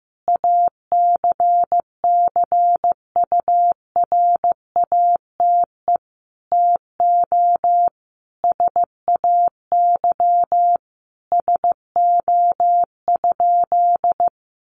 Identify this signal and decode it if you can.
Morse code, heard as a keyed tone, about 15 wpm, 700 Hz. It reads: ACCURATE TO SAY SO?
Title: Sentences from Top 1200 Words 15wpm